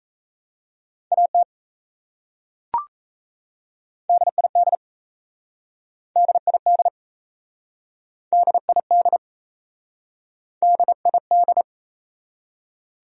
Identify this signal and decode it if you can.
Morse code, heard as a keyed tone, about 37 wpm, 700 Hz. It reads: AT DID DID DID DID